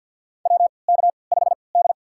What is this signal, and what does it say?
Morse code, heard as a keyed tone, about 53 wpm, spaced as for 29 wpm, 700 Hz. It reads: WK5D
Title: USA Call Signs - 3x Character Spacing - 50wpm